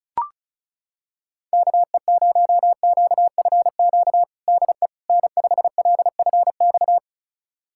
Morse code, heard as a keyed tone, about 36 wpm, 690 Hz. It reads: KE0QFQ DE N5LFX
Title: USA Callsign DE Callsign 35wpm